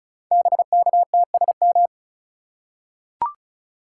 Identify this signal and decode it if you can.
Morse code, heard as a keyed tone, about 36 wpm, 690 Hz. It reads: BKTSM
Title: Code Groups - 5 Letters - 35wpm